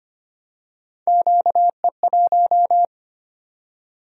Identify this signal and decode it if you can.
Morse code, heard as a keyed tone, about 25 wpm, 700 Hz. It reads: QE1
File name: Three Letter-Number Combinations - 25wpm